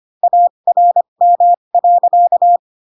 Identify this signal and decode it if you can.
Morse code, heard as a keyed tone, about 25 wpm, spaced as for 22 wpm, 700 Hz. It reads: ARM.
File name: Pinocchio / Pinocchio ch7 18 25 ik1hge